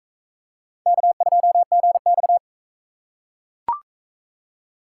Morse code, heard as a keyed tone, about 44 wpm, 700 Hz. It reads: K2GX